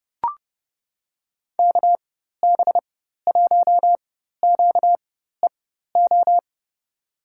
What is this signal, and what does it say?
Morse code, heard as a keyed tone, about 30 wpm, spaced as for 14 wpm, 700 Hz. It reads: KB1QEO
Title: USA Call Signs - 6 Characters - 4x Character Spacing - 30wpm